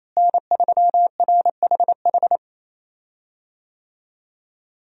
Morse code, heard as a keyed tone, about 28 wpm, 700 Hz. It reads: N3RHH
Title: USA Call Signs - 5 Characters - Repeated 2x - 28wpm